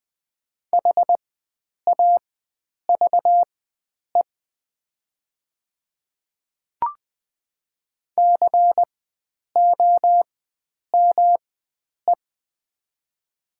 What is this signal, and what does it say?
Morse code, heard as a keyed tone, about 20 wpm, spaced as for 9 wpm, 700 Hz. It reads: HAVE COME
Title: 4x Character Spacing - Top 100 Words - 20wpm